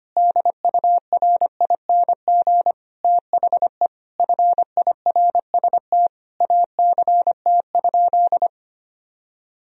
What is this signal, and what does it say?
Morse code, heard as a keyed tone, about 25 wpm, 700 Hz. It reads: DURING THE FIRST ACT?
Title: Sentences from Top 1000 Words - Review Part 2 - 25wpm